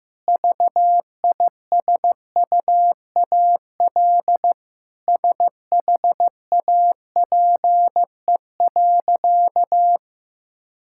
Morse code, heard as a keyed tone, about 15 wpm, 700 Hz. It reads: VISUAL SHAPE.
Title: Sentences from Top 1000 Words - Review Part 6 - 15wpm